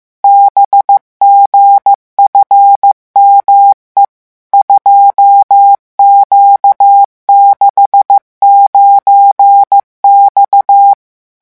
Morse code, heard as a keyed tone, about 15 wpm, 800 Hz. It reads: BGFME 2Q69X
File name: morse mp3 / revision13-vitesse 14 mots